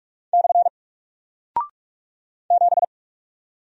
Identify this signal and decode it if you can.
Morse code, heard as a keyed tone, about 44 wpm, 700 Hz. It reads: / 7
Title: Single Character (Letter:Number:Punctuation:Prosign) 45wpm